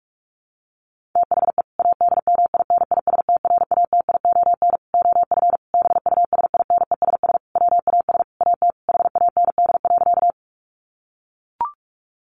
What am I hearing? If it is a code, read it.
Morse code, heard as a keyed tone, about 44 wpm, 700 Hz. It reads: THE ADMINISTRATION OF BUSINESS WAS AT HAND.
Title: Sentences from Top 1300 Words 45wpm